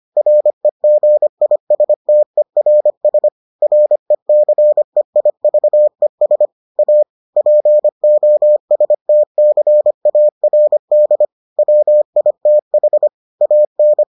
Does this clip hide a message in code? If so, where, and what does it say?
Yes: Morse code from 0.2 to 14.0 s, heard as a keyed tone, about 25 wpm, 600 Hz. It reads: REGISTERS RECEIVES A POSTCARD WITH AN